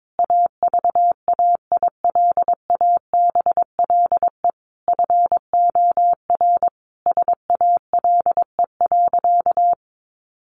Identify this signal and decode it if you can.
Morse code, heard as a keyed tone, about 22 wpm, 700 Hz. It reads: AVAILABLE FOR SALE.
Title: Sentences from Top 1400 Words 22wpm